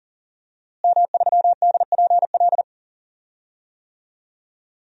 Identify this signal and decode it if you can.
Morse code, heard as a keyed tone, about 38 wpm, 700 Hz. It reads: M3DPL
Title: DXCC-100 Call Signs - 5 Characters - Repeated 2x - 40wpm